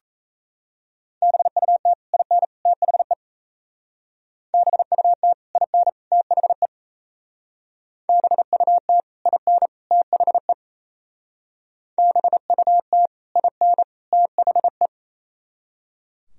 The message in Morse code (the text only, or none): BUT IN THE BUT IN THE BUT IN THE BUT IN THE